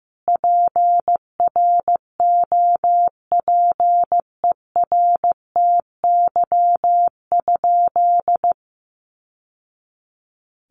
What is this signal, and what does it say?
Morse code, heard as a keyed tone, about 15 wpm, 700 Hz. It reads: PROPERTY?
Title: Sentences from Top 1000 Words 15wpm